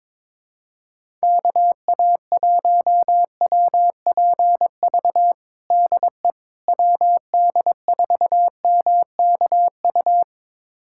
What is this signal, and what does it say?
Morse code, heard as a keyed tone, about 22 wpm, 700 Hz. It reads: KA1WPV DE WD4MKU
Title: USA Callsign DE Callsign - 6 Characters - 22wpm